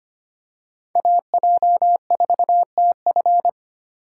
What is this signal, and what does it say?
Morse code, heard as a keyed tone, about 25 wpm, 700 Hz. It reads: AJ4TF
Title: K1USN SST QSO Practice - 25wpm